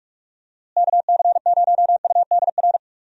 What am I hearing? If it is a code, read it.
Morse code, heard as a keyed tone, about 43 wpm, 700 Hz. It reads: KC0UDR
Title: USA Call Signs - 6 Characters - Repeated 2x - 45wpm